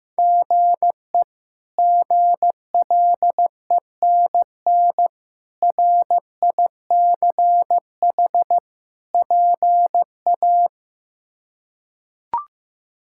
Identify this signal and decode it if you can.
Morse code, heard as a keyed tone, about 15 wpm, 700 Hz. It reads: GE GLENN RICH PA
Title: K1USN SST QSO Practice - 15wpm